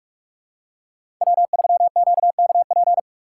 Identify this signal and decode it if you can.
Morse code, heard as a keyed tone, about 46 wpm, 700 Hz. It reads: W3QKP
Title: USA Call Signs - 8x Character Spacing - 45wpm